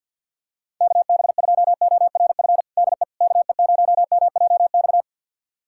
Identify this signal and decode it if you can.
Morse code, heard as a keyed tone, about 50 wpm, 700 Hz. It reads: KB2ORF DE KE0MJX
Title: USA Callsign DE Callsign - 6 Characters - 50wpm